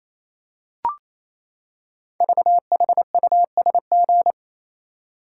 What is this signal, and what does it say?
Morse code, heard as a keyed tone, about 27 wpm, 700 Hz. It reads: VHUSG
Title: Code Groups - 5 Letters - 28wpm